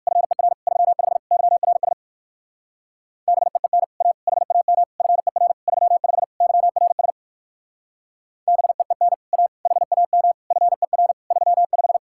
Morse code, heard as a keyed tone, about 57 wpm, 700 Hz. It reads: FER 35 YRS BEEN A HAM FER 35 YRS BEEN A HAM FER 35